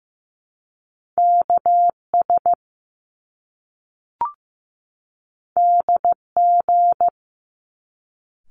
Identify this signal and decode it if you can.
Morse code, heard as a keyed tone, about 15 wpm, 700 Hz. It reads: KS DG
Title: Top Two Letter Combinations in Words 15wpm